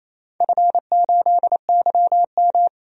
Morse code, heard as a keyed tone, about 28 wpm, 700 Hz. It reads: F8YM
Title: DXCC-100 Call Signs - 4 Characters - 28wpm